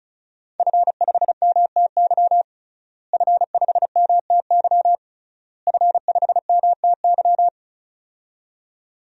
Morse code, heard as a keyed tone, about 36 wpm, 700 Hz. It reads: F5MTY F5MTY F5MTY
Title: DXCC-100 Call Signs - 5 Characters - Repeated 3x - 35wpm